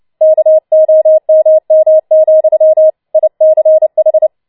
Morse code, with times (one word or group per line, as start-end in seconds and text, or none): 0.2-2.9 KOMM,
3.1-4.3 ICH